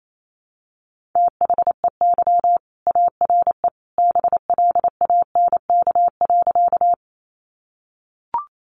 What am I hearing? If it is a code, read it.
Morse code, heard as a keyed tone, about 28 wpm, 700 Hz. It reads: THEY ARE BLANK.
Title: Sentences from Top 1500 Words 28wpm